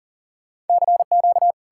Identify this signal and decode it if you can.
Morse code, heard as a keyed tone, about 52 wpm, 700 Hz. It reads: CQ